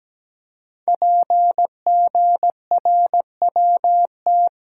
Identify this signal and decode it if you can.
Morse code, heard as a keyed tone, about 17 wpm, 700 Hz. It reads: PGRWT
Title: Code Groups - 5 Letters - 17wpm